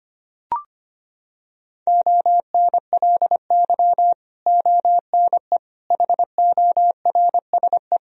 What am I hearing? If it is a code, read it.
Morse code, heard as a keyed tone, about 25 wpm, 700 Hz. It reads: ONLY ONE HORSE
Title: Sentences from Top 1000 Words - Review Part 1 - 25wpm